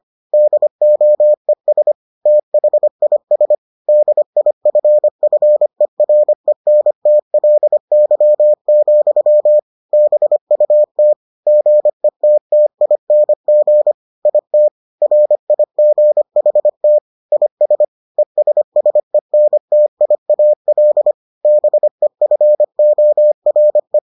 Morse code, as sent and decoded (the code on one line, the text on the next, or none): -.. --- . ... / - .... .. ... / -.. .. ..-. ..-. .-. -. - .-.. -.-- --..-- / -... ..- - / --. - - .. -. --. / .. - / .-. .. --. .... - / .. ... / . ... ... -. - .. .- .-.. / -... ..-. --- .-. .
DOES THIS DIFFERENTLY, BUT GETTING IT RIGHT IS ESSENTIAL BEFORE